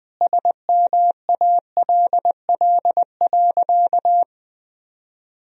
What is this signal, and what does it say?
Morse code, heard as a keyed tone, about 21 wpm, 700 Hz. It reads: SMALL.
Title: Sentences from Top 1000 Words - Review Part 1 - 20wpm